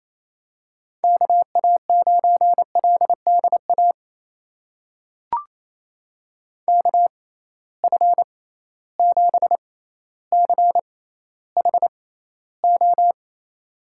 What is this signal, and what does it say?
Morse code, heard as a keyed tone, about 28 wpm, spaced as for 14 wpm, 700 Hz. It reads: KA9LDA KF7CHO
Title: USA Call Signs - 6 Characters - 6x Character Spacing - 28wpm